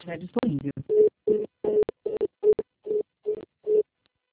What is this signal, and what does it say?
Congestion tone (425 Hz), with speech.
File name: out- -2009- - - .157